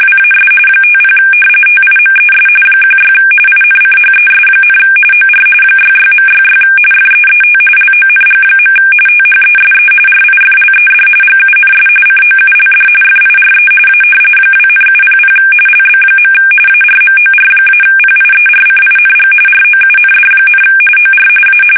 RTTY, about 45.45 baud. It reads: WAS 50 TIMES 16 INCHES, OR 66.66 FEET LONG. THESE RIFLED GUNS COULD FIRE A 2,700 LB. ARMOR-PIERCING ROUND, OR A
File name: fsk850 / battleshipnj